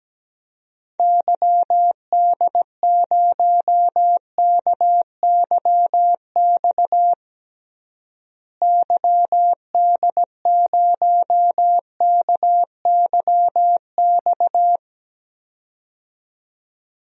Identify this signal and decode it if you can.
Morse code, heard as a keyed tone, about 17 wpm, 700 Hz. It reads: YD0KYX YD0KYX
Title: DXCC-100 Call Signs - 6 Characters - Repeated 2x - 17wpm